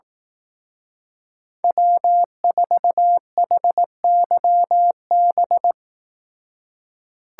Morse code, heard as a keyed tone, about 18 wpm, 700 Hz. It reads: W4HYB
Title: Speed Racing - USA 1x3 Call Signs 15wpm